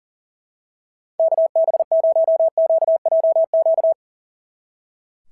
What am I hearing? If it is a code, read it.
Morse code, heard as a keyed tone, about 42 wpm, 690 Hz. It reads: KB0QJQ
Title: SOTA QSO Practice 40wpm